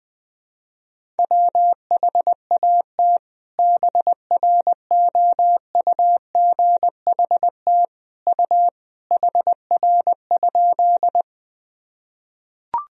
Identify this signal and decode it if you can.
Morse code, heard as a keyed tone, about 20 wpm, 700 Hz. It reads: WHAT BROUGHT U HR?
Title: Sentences from Top 500 Words and Abbreviations 20wpm